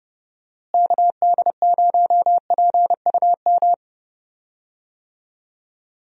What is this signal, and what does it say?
Morse code, heard as a keyed tone, about 30 wpm, 700 Hz. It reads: KD0PUM
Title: USA Call Signs - 6 Characters - Repeated 3x - 30wpm